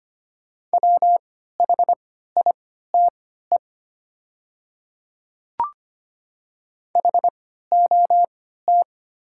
Morse code, heard as a keyed tone, about 25 wpm, spaced as for 14 wpm, 700 Hz. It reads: WHITE HOT
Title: 3x Character Spacing - Top 300 Words - 25wpm